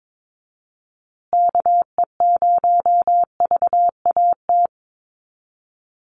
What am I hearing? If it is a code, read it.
Morse code, heard as a keyed tone, about 22 wpm, 700 Hz. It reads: KE0VAT
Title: USA Call Signs Repeated 2x 22wpm